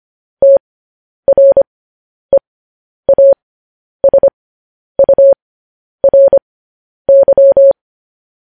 Morse code, heard as a keyed tone, about 25 wpm, spaced as for 10 wpm, 550 Hz. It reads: TREASURY
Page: news headlines in morse code at 10 wpm